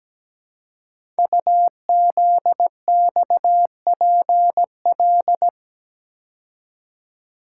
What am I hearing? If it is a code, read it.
Morse code, heard as a keyed tone, about 17 wpm, 700 Hz. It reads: UZXPL
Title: Code Groups - 5 Letters - 17wpm